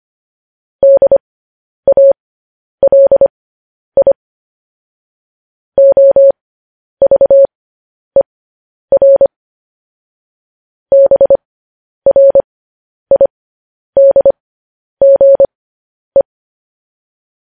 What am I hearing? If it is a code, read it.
Morse code, heard as a keyed tone, about 25 wpm, spaced as for 10 wpm, 550 Hz. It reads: DALI OVER BRIDGE